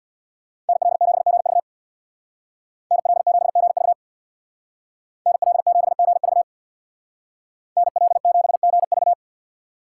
Morse code, heard as a keyed tone, about 64 wpm, 700 Hz. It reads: NL7GV NL7GV NL7GV NL7GV